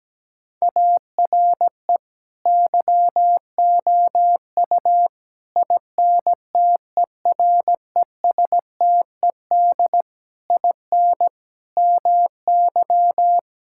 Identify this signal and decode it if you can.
Morse code, heard as a keyed tone, about 17 wpm, 700 Hz. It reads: ARE YOU INTERESTED IN MY
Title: Sentences from Top 1000 Words - Review Part 6 - 17wpm